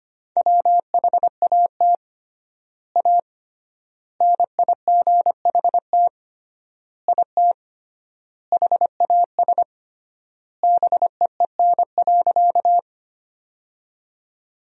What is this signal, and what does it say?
Morse code, heard as a keyed tone, about 25 wpm, 700 Hz. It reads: WHAT A NIGHT IT HAS BEEN.
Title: 3x Word-Spacing - Sentences from Top 300 Words - 25wpm